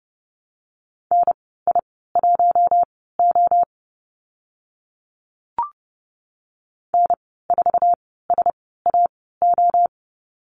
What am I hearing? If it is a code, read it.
Morse code, heard as a keyed tone, about 30 wpm, spaced as for 17 wpm, 700 Hz. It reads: NI1O N4SAO
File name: Lesson 15g - Number 5 - Call Signs 30wpm